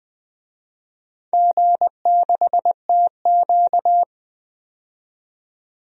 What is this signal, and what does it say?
Morse code, heard as a keyed tone, about 20 wpm, 700 Hz. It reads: G6TQ
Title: DXCC-100 Call Signs - 4 Characters - 20wpm